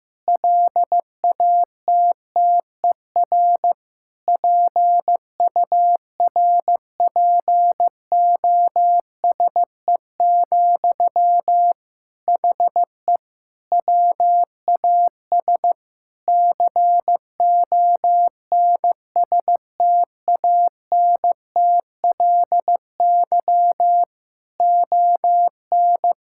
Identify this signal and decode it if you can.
Morse code, heard as a keyed tone, about 15 wpm, 700 Hz. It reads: LATTER PURPOSE, HE WAS CONSTANTLY ON
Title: Sentences from Top 1600 Words 15wpm